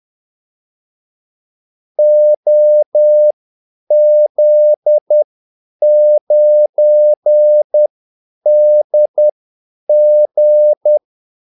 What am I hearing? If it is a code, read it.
Morse code, heard as a keyed tone, about 10 wpm, spaced as for 8 wpm, 600 Hz. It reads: OZ9DG
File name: Callsigns-10-8wpm